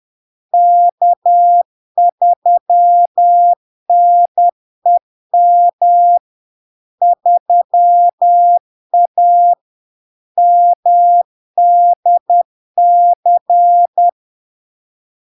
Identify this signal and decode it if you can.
Morse code, heard as a keyed tone, about 10 wpm, 700 Hz. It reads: K3NEM 3A MDC
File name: LARK-FD-2022-Log-in-CW-10wpm-700-Hz